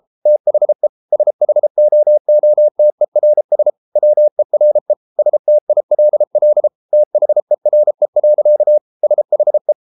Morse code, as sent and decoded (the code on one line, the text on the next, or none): - .... / ... .... --- --- - . .-. ... / .-- . .-. / ... - .. .-.. .-.. / - .... .-. .-.-.- / ... .... .
THE SHOOTERS WERE STILL THERE. SHE